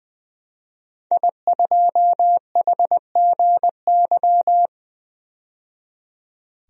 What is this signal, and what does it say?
Morse code, heard as a keyed tone, about 21 wpm, 700 Hz. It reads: I2HGY
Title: DXCC-100 Call Signs - 5 Characters - 20wpm